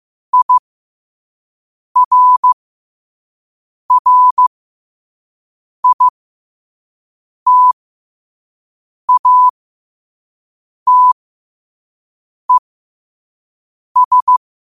Morse code, heard as a keyed tone, about 15 wpm, spaced as for 5 wpm, 1000 Hz. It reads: IRRITATES
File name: 5wpm